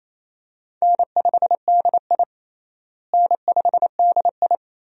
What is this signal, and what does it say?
Morse code, heard as a keyed tone, about 28 wpm, 700 Hz. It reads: N5DI N5DI
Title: USA Call Signs - 4 Characters - Repeated 2x - 28wpm